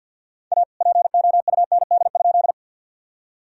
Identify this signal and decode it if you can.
Morse code, heard as a keyed tone, about 50 wpm, 700 Hz. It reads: A POUND?
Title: Sentences from Top 1000 Words - Review Part 2 - 50wpm